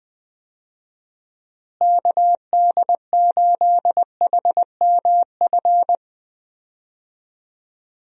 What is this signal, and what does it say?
Morse code, heard as a keyed tone, about 20 wpm, 700 Hz. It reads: KD8HMF
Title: USA Call Signs - 6 Characters - Repeated 2x - 20wpm